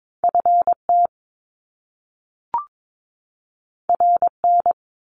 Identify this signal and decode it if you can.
Morse code, heard as a keyed tone, about 22 wpm, 700 Hz. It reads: FT RN
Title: Top Two Letter Combinations in Words 22wpm